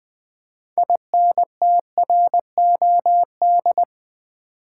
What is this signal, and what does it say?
Morse code, heard as a keyed tone, about 20 wpm, 700 Hz. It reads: INTROD
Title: Build A Word - 3 Letters At A Time - 9-Letter Words - 20wpm